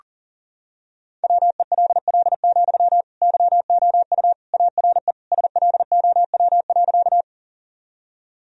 Morse code, heard as a keyed tone, about 40 wpm, 700 Hz. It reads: WELL, YOU ARE SLOW.
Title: Sentences from Top 1000 Words - Review Part 2 - 40wpm